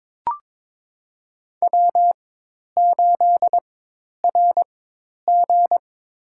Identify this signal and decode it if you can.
Morse code, heard as a keyed tone, about 22 wpm, spaced as for 10 wpm, 700 Hz. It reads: W8RG
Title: USA Call Signs - 4 Characters - 4x Character Spacing - 22wpm